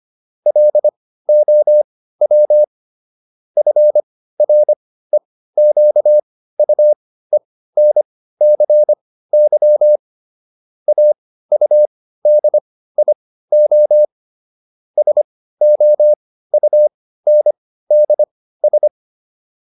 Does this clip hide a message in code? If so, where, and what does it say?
Yes: Morse code from 0.5 to 18.9 s, heard as a keyed tone, about 25 wpm, spaced as for 15 wpm, 600 Hz. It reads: LOW FREQUENCY AUDIO SOUNDS